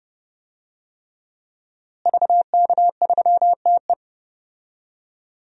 Morse code, heard as a keyed tone, about 30 wpm, 700 Hz. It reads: VK3TE
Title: DXCC-100 Call Signs - 5 Characters - Repeated 3x - 30wpm